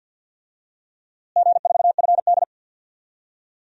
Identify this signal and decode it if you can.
Morse code, heard as a keyed tone, about 46 wpm, 700 Hz. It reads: G4FD
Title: DXCC-100 Call Signs - 4 Characters - Repeated 2x - 50wpm